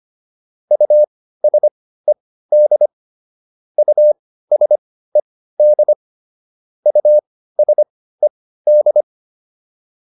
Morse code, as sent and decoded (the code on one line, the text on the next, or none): ..- ... -.. / ..- ... -.. / ..- ... -..
USED USED USED